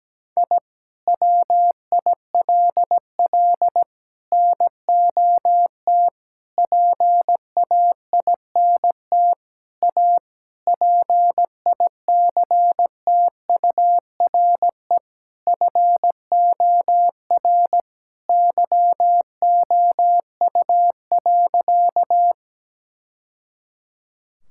Morse code, heard as a keyed tone, about 17 wpm, 700 Hz. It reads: I WILL NOT PAINT A PICTURE FOR YOU.
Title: Sentences from Top 1000 Words - Review Part 6 - 17wpm